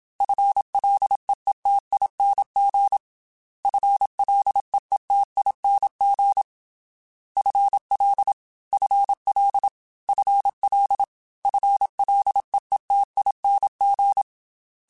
Morse code, heard as a keyed tone, about 27 wpm, 800 Hz. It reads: FLEETING FLEETING FL FL FL FLEETING
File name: FL / FL mp3s / FL Prefixes 3x2 Twice @25